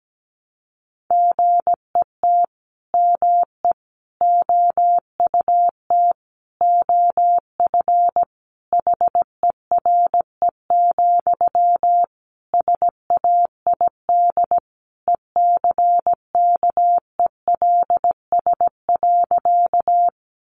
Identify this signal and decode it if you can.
Morse code, heard as a keyed tone, about 17 wpm, 700 Hz. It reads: GET ME OUT OF HERE, SAID ECKELS.
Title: A Sound Of Thunder 17wpm